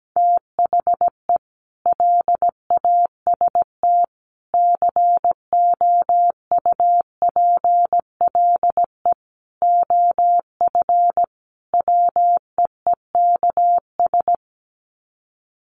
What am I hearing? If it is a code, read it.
Morse code, heard as a keyed tone, about 17 wpm, 700 Hz. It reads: THE LAST COUPLE OF WEEKS
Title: Sets of 5 Words Encore2 17wpm